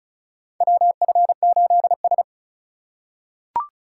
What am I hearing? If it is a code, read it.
Morse code, heard as a keyed tone, about 34 wpm, 700 Hz. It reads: WF8S
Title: USA Call Signs - 4 Characters - 3x Character Spacing - 35wpm